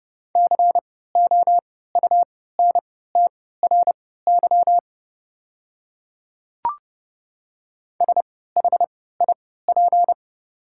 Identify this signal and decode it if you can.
Morse code, heard as a keyed tone, about 30 wpm, spaced as for 17 wpm, 690 Hz. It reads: COUNTRY SHIP
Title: 3x Character Spacing - Top 500 Words - 30wpm